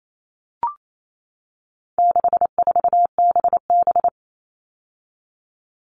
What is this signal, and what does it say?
Morse code, heard as a keyed tone, about 28 wpm, 700 Hz. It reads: 64BB
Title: All Cut Numbers - 4-Digit 28wpm